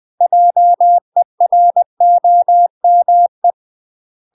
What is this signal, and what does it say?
Morse code, heard as a keyed tone, about 20 wpm, 700 Hz. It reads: JEROME
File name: UK MASTER QSOs TRAINING/20WPM